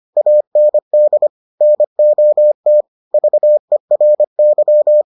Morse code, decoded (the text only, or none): AND NOT VERY